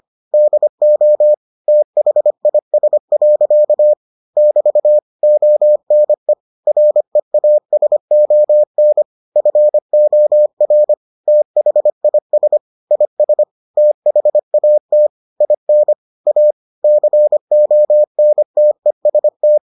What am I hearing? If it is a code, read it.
Morse code, heard as a keyed tone, about 25 wpm, 600 Hz. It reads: DO THIS. <BT> ONE REASON FOR THIS IS THAT IN A CONTEST